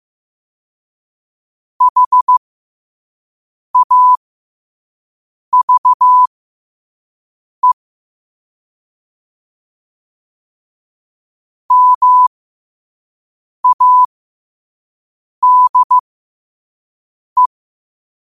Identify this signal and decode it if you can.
Morse code, heard as a keyed tone, about 15 wpm, spaced as for 5 wpm, 1000 Hz. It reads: HAVE MADE
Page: New quotes every day in morse code at 5 Words per minute.